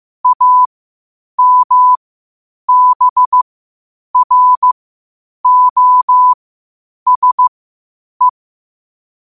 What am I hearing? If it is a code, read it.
Morse code, heard as a keyed tone, about 15 wpm, spaced as for 8 wpm, 1000 Hz. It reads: AMBROSE
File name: qotdfiles / 8wpm